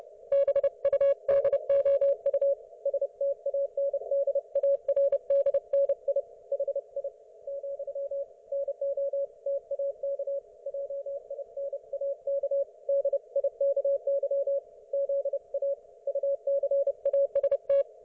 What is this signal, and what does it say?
Morse code, heard as a keyed tone, about 30 wpm, 560 Hz. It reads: BUDOU STANDARDNI HI , NO TAK JINAK DIKY ZA UCAST
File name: Archiv / CUC208 / ok1fhi cul208